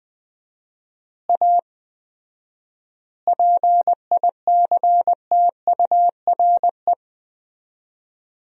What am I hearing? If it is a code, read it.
Morse code, heard as a keyed tone, about 21 wpm, 700 Hz. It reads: A PICTURE
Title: 4x Word-Spacing - Sentences from Top 500 Words - 20wpm